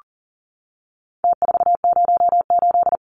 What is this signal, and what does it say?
Morse code, heard as a keyed tone, about 40 wpm, 700 Hz. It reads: T408
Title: Numbers - Contest Style - 4-Digit 40wpm